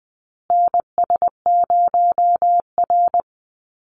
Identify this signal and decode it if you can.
Morse code, heard as a keyed tone, about 20 wpm, 700 Hz. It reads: NS0R
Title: Call Signs Repeated 2x - 20wpm